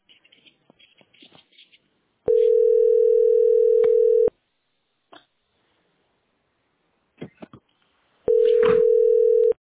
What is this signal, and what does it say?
Ringback tone (North America), 440 + 480 Hz.
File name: 64b966c3-ec1e-4fbd-8785